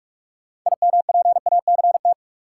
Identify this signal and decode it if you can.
Morse code, heard as a keyed tone, about 46 wpm, 700 Hz. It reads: IMPACT
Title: Sentences from Top 1000 Words - Review Part 6 - New Word Drill - 45wpm